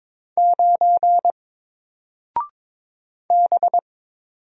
Morse code, heard as a keyed tone, about 22 wpm, 700 Hz. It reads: 9 B
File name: Single-Letter-Number-22wpm